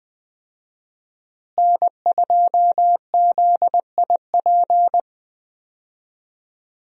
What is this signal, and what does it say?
Morse code, heard as a keyed tone, about 20 wpm, 700 Hz. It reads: N2ZIP
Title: USA Call Signs - 5 Characters - Repeated 3x - 20wpm